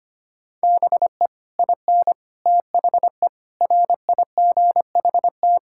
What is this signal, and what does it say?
Morse code, heard as a keyed tone, about 25 wpm, 700 Hz. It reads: BE IN THE RIGHT
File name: Sets of 9 Words - 25wpm